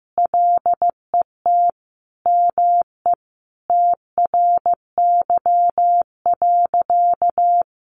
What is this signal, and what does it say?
Morse code, heard as a keyed tone, about 15 wpm, 700 Hz. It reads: LET ME TRY.
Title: Sentences from Top 600 Words 15wpm